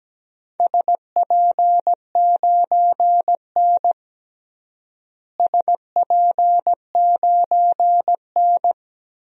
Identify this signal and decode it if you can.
Morse code, heard as a keyed tone, about 17 wpm, 700 Hz. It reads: SP9N SP9N
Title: DXCC-100 Call Signs - 4 Characters - Repeated 3x - 17wpm